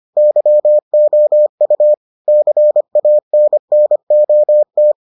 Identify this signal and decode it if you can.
Morse code, heard as a keyed tone, about 25 wpm, 600 Hz. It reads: YOU CANNOT